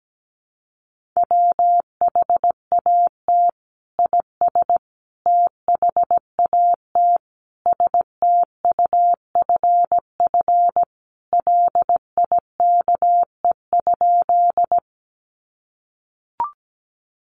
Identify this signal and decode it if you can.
Morse code, heard as a keyed tone, about 17 wpm, 700 Hz. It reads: WHAT IS THAT STUFF LIKE?
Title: Sentences from Top 1400 Words 17wpm